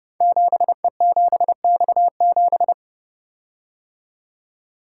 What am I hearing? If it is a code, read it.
Morse code, heard as a keyed tone, about 30 wpm, 700 Hz. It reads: 7E7X7
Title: Code Groups - 5 Letter:Numbers - 30wpm